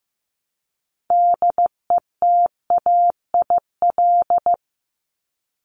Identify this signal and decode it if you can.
Morse code, heard as a keyed tone, about 15 wpm, 700 Hz. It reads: DETAIL
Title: Top 1251-1500 Words 15wpm